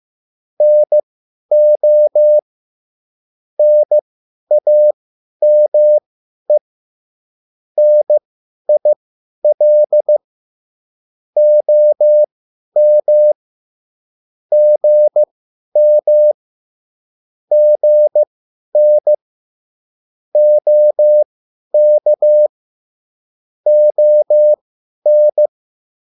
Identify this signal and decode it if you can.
Morse code, heard as a keyed tone, about 15 wpm, spaced as for 10 wpm, 600 Hz. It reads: NO NAME NIL OM GM GN OK ON